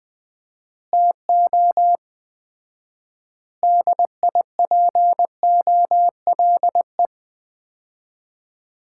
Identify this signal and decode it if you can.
Morse code, heard as a keyed tone, about 20 wpm, 700 Hz. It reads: TO DIPOLE
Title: Common QSO Phrases - 4x Word-Spacing - 20wpm